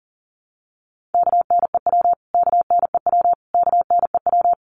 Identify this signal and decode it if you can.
Morse code, heard as a keyed tone, about 40 wpm, 700 Hz. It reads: KNEW KNEW KNEW
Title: Top 500 Words Repeated 40wpm